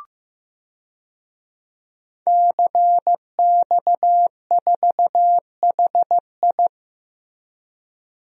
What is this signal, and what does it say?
Morse code, heard as a keyed tone, about 15 wpm, 700 Hz. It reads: CX4HI
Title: DXCC-100 Call Signs - 5 Characters - Repeated 2x - 15wpm